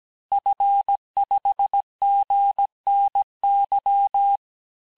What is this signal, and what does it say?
Morse code, heard as a keyed tone, about 17 wpm, 800 Hz. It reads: F5GNY
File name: corrige-cours-1581-fin-du-QSO-vitesse-16-mpm